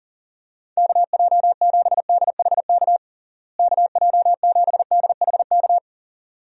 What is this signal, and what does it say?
Morse code, heard as a keyed tone, about 40 wpm, 700 Hz. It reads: KJ7DHK KJ7DHK
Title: USA Call Signs - 6 Characters - Repeated 2x - 40wpm